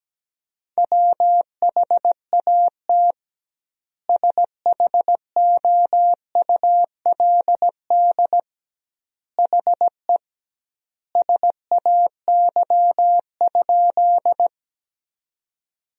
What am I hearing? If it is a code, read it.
Morse code, heard as a keyed tone, about 17 wpm, 700 Hz. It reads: WHAT SHOULD HE SAY?
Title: 2x Word-Spacing - Sentences from Top 300 Words - 17wpm